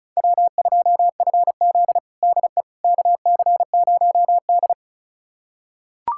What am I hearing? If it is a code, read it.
Morse code, heard as a keyed tone, about 34 wpm, 700 Hz. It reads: W2FZ DE KC0D
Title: USA Callsign DE Callsign 35wpm